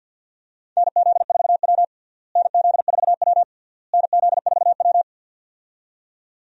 Morse code, heard as a keyed tone, about 52 wpm, 700 Hz. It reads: NZ4W NZ4W NZ4W
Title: USA Call Signs - 4 Characters - Repeated 3x - 50wpm